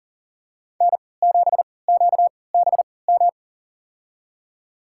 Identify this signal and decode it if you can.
Morse code, heard as a keyed tone, about 40 wpm, spaced as for 23 wpm, 700 Hz. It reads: N7QBM
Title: USA Call Signs - 5 Characters - 3x Character Spacing - 40wpm